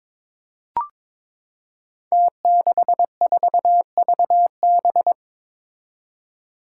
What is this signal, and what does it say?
Morse code, heard as a keyed tone, about 22 wpm, 700 Hz. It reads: T64VB
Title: Five Challenging Letter-Number Combinations 22wpm